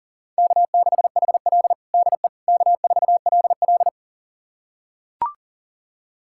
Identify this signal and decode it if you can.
Morse code, heard as a keyed tone, about 40 wpm, 700 Hz. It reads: K6HL DE K4LL
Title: USA Callsign DE Callsign - 4 Characters - 40wpm